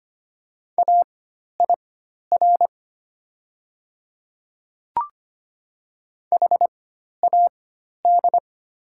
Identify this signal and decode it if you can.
Morse code, heard as a keyed tone, about 25 wpm, spaced as for 12 wpm, 690 Hz. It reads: AIR HAD
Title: 4x Character Spacing - Top 200 Words - 25wpm